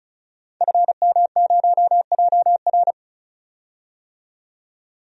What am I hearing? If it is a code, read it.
Morse code, heard as a keyed tone, about 34 wpm, 700 Hz. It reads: FM0JR
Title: DXCC-100 Call Signs - 5 Characters - 35wpm